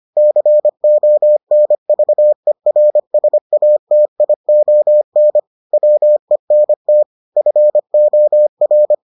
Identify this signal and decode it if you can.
Morse code, heard as a keyed tone, about 25 wpm, 600 Hz. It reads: CONVERSATION WENT FOR